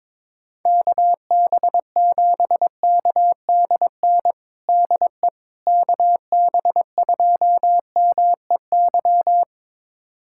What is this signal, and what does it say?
Morse code, heard as a keyed tone, about 23 wpm, 700 Hz. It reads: KB7KDN DE KB2MEY
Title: USA Callsign DE Callsign - 6 Characters - 22wpm